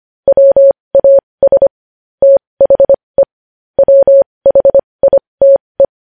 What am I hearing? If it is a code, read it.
Morse code, heard as a keyed tone, about 25 wpm, spaced as for 20 wpm, 550 Hz. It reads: WAS THE WHITE
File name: podcast / News headlines-1x-25x20